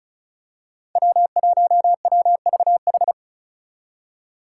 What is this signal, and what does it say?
Morse code, heard as a keyed tone, about 34 wpm, 700 Hz. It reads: W1WVH
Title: Five Challenging Letter-Number Combinations 35wpm